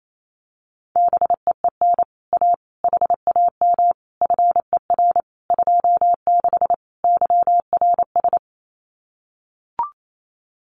Morse code, heard as a keyed tone, about 28 wpm, 700 Hz. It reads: BEEN A HAM FER 26 YRS
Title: Common QSO Phrases Encore 28wpm